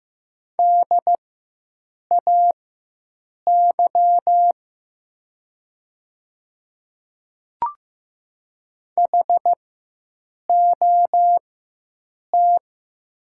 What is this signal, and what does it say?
Morse code, heard as a keyed tone, about 15 wpm, spaced as for 7 wpm, 700 Hz. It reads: DAY HOT
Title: 4x Character Spacing - Top 200 Words - 15wpm